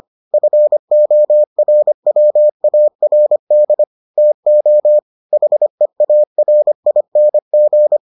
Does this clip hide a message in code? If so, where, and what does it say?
Yes: Morse code from 0.3 to 8.0 s, heard as a keyed tone, about 25 wpm, 600 Hz. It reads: FORWARD TO HEARING